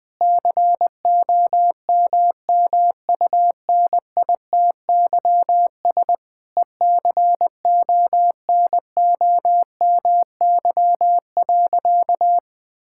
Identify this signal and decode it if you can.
Morse code, heard as a keyed tone, about 20 wpm, 700 Hz. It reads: COMMUNITYS ECONOMY.
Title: Difficult Sentences with Hints - 20wpm